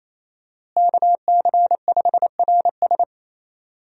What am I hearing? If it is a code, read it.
Morse code, heard as a keyed tone, about 27 wpm, 700 Hz. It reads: KC5RS
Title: USA Call Signs - 5 Characters - Repeated 2x - 28wpm